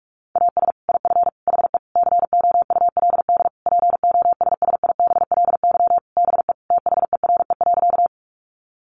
Morse code, heard as a keyed tone, about 44 wpm, 700 Hz. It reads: AS IF HE COULD POSSIBLY BE THERE.
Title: Sentences from Top 1300 Words 45wpm